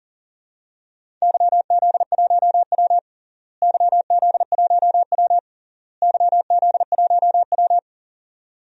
Morse code, heard as a keyed tone, about 40 wpm, 700 Hz. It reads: YZ1W YZ1W YZ1W
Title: DXCC-100 Call Signs - 4 Characters - Repeated 3x - 40wpm